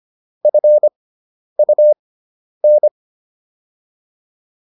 Morse code, heard as a keyed tone, about 25 wpm, spaced as for 10 wpm, 600 Hz. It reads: FUN